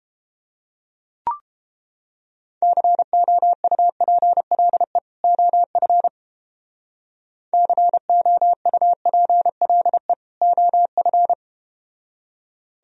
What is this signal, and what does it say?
Morse code, heard as a keyed tone, about 31 wpm, 700 Hz. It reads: COUPLE OF COUPLE OF